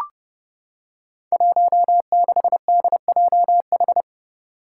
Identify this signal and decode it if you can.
Morse code, heard as a keyed tone, about 30 wpm, 700 Hz. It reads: 16DJH
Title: Five Challenging Letter-Number Combinations 30wpm